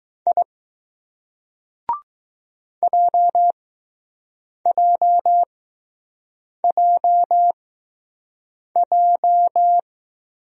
Morse code, heard as a keyed tone, about 20 wpm, spaced as for 6 wpm, 700 Hz. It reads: I JJJJ